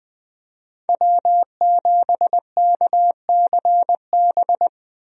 Morse code, heard as a keyed tone, about 20 wpm, 700 Hz. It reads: W7KCB
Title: USA Call Signs - 5 Characters - 20wpm